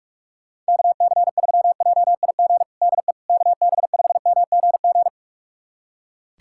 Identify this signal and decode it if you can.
Morse code, heard as a keyed tone, about 42 wpm, 700 Hz. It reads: KC3JIG DE KB5MGG
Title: USA Callsign DE Callsign 45wpm